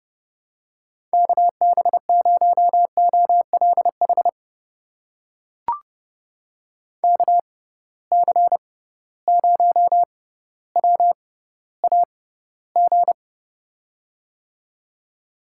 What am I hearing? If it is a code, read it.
Morse code, heard as a keyed tone, about 30 wpm, spaced as for 15 wpm, 700 Hz. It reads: KB0OLH KC0WAG
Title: USA Call Signs - 6x Character Spacing - 30wpm